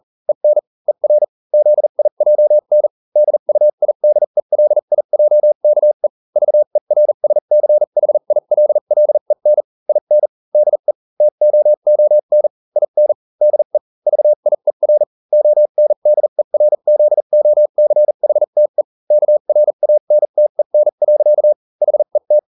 Morse code, heard as a keyed tone, about 40 wpm, spaced as for 35 wpm, 600 Hz. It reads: EN ER ZIJN DUIDELIJKE VERSCHILLEN IN DE TOON IN DE VIER ONDERZOCHTE KRANTEN. HET